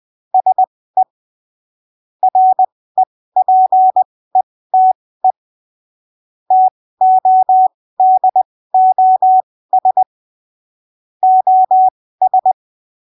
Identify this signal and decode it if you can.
Morse code, heard as a keyed tone, about 20 wpm, spaced as for 15 wpm, 750 Hz. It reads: SE REPETE TODOS OS
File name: arquivos / texto / texto 08 20ppm